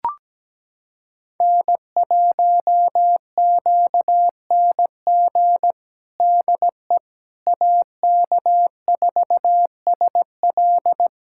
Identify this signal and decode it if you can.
Morse code, heard as a keyed tone, about 17 wpm, 700 Hz. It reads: N1QNG DE AK4SL
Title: USA Callsign DE Callsign - 5 Characters - 17wpm